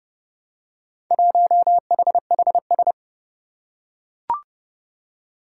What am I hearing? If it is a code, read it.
Morse code, heard as a keyed tone, about 28 wpm, 700 Hz. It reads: 1HHS
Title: Four Challenging Letter-Number Combinations 30wpm